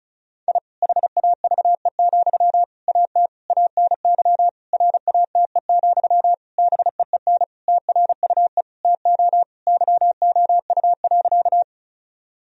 Morse code, heard as a keyed tone, about 36 wpm, 700 Hz. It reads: I HAVE, AT ANY RATE, BEEN TRUE TO YOU.
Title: Sentences from Top 1000 Words - Review Part 4 - 35wpm